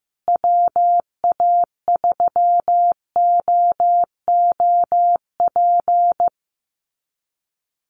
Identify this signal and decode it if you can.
Morse code, heard as a keyed tone, about 15 wpm, 700 Hz. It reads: WA3OOP
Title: USA Call Signs Repeated 3x 15wpm